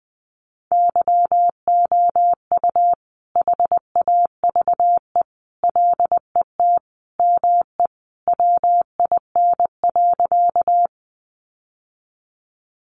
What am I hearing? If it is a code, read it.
Morse code, heard as a keyed tone, about 20 wpm, 700 Hz. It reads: YOU HAVE LET ME WIN.
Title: Sentences from Top 1500 Words 20wpm